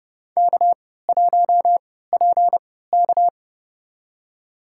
Morse code, heard as a keyed tone, about 30 wpm, spaced as for 17 wpm, 700 Hz. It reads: K1PK
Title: USA Call Signs - 3x Character Spacing - 30wpm